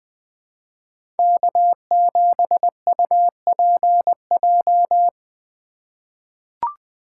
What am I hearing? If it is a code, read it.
Morse code, heard as a keyed tone, about 21 wpm, 700 Hz. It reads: K7UPJ
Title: K1USN SST QSO Practice - 20wpm